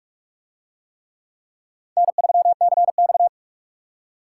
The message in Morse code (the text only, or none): N3CX